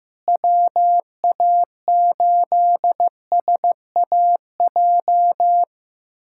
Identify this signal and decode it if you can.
Morse code, heard as a keyed tone, about 15 wpm, 700 Hz. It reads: WA8SAJ
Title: USA Callsign DE Callsign - 6 Characters - 15wpm